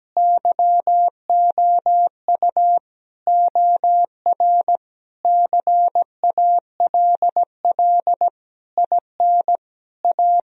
Morse code, heard as a keyed tone, about 17 wpm, 700 Hz. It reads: YOU OR CALL IN A
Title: Sentences from Top 1000 Words - Review Part 1 - 17wpm